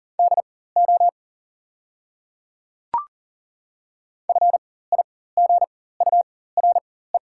Morse code, heard as a keyed tone, about 40 wpm, spaced as for 19 wpm, 700 Hz. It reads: DO FIGURE
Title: 4x Character Spacing - Top 500 Words - 40wpm